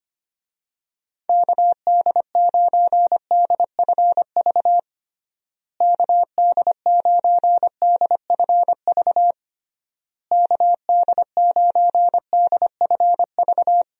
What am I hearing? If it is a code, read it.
Morse code, heard as a keyed tone, about 25 wpm, 700 Hz. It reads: KD9DFV KD9DFV KD9DFV
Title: USA Call Signs - 6 Characters - Repeated 3x - 25wpm